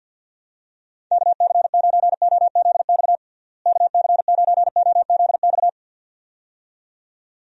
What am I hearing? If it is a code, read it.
Morse code, heard as a keyed tone, about 50 wpm, 700 Hz. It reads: KC9OZX KC9OZX
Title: USA Call Signs - 6 Characters - Repeated 2x - 50wpm